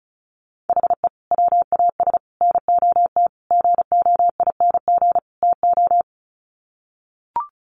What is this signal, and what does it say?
Morse code, heard as a keyed tone, about 34 wpm, 700 Hz. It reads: HE WAS NOT GOING TO
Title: Sets of 5 Words 35wpm